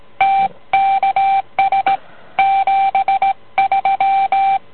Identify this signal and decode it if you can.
Morse code, heard as a keyed tone, about 17 wpm, 780 Hz. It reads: TKS 73